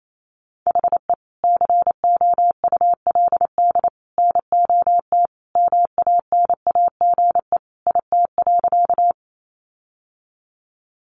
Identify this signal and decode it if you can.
Morse code, heard as a keyed tone, about 28 wpm, 700 Hz. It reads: HE COULD NOT MANAGE IT.
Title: Sentences from Top 1400 Words 28wpm